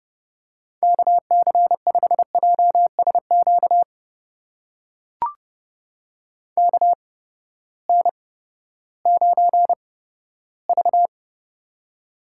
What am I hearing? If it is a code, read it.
Morse code, heard as a keyed tone, about 31 wpm, spaced as for 15 wpm, 700 Hz. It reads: KC5JSQ KN9V
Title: USA Call Signs - 8x Character Spacing - 30wpm